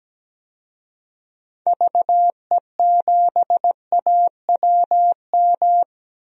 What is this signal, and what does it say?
Morse code, heard as a keyed tone, about 17 wpm, 700 Hz. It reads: VE7AWM
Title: DXCC-100 Call Signs - 6 Characters - 17wpm